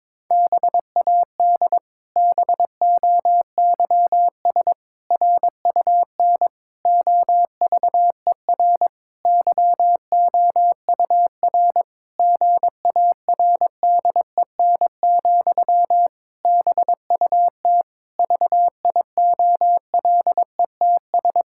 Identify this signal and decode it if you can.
Morse code, heard as a keyed tone, about 21 wpm, 700 Hz. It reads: BAD BOYS RUN OVER YOUR GARDEN, BUT VIOLETS
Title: Pangram Challenge v2 - 22wpm